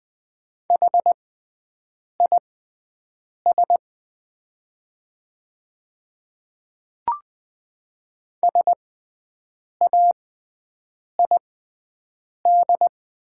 Morse code, heard as a keyed tone, about 20 wpm, spaced as for 7 wpm, 700 Hz. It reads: HIS SAID
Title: 6x Character Spacing - Top 200 Words - 20wpm